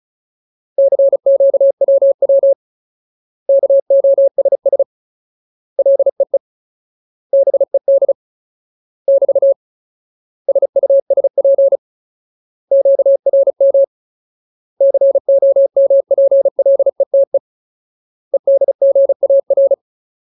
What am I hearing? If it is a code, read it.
Morse code, heard as a keyed tone, about 34 wpm, 550 Hz. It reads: CQWW KOSS LEE BED <BT> SUSP QRM COMPLETE EDGAR
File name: Ultimate Word List / 35WPM Ultimate